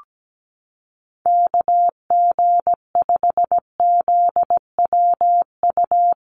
Morse code, heard as a keyed tone, about 17 wpm, 700 Hz. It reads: KG5ZWU
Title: USA Call Signs Repeated 3x 17wpm